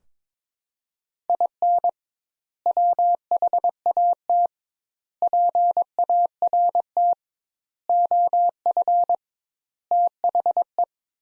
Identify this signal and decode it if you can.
Morse code, heard as a keyed tone, about 21 wpm, 700 Hz. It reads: IN WHAT PART OF THE
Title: Sentences Top 200 Words - 22wpm